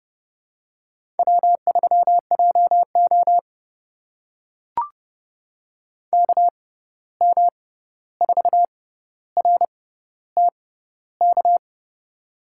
Morse code, heard as a keyed tone, about 30 wpm, spaced as for 14 wpm, 700 Hz. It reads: W3JO KM4RTK
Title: USA Call Signs - 6x Character Spacing - 30wpm